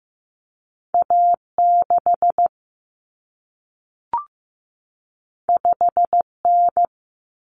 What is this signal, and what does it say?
Morse code, heard as a keyed tone, about 15 wpm, 700 Hz. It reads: A6 5N
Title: Common Cut Numbers - 2-Digit 15wpm